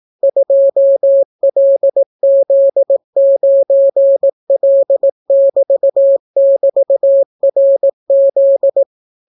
Morse code, heard as a keyed tone, about 18 wpm, 550 Hz. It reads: 2LZ9L<BT><BT>RZ